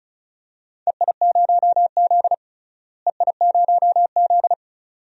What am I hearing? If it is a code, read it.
Morse code, heard as a keyed tone, about 36 wpm, 700 Hz. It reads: EI0Z EI0Z
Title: DXCC-100 Call Signs - 4 Characters - Repeated 2x - 35wpm